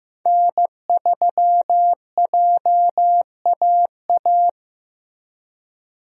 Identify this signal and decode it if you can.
Morse code, heard as a keyed tone, about 15 wpm, 700 Hz. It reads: N3JAA
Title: USA Call Signs - 5 Characters - Repeated 2x - 15wpm